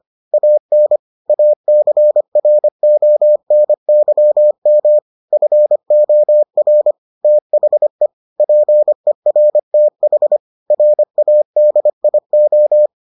Morse code, heard as a keyed tone, about 25 wpm, 600 Hz. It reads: AN ACRONYM FOR THE PERTH RADIO